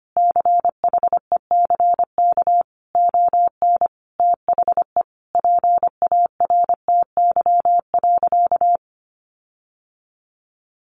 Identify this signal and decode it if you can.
Morse code, heard as a keyed tone, about 25 wpm, 700 Hz. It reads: CHECK ON THE PARTY.
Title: Sentences from Top 1400 Words 25wpm